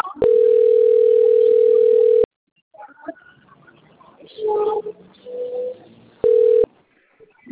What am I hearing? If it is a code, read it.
Ringback tone (North America), 440 + 480 Hz.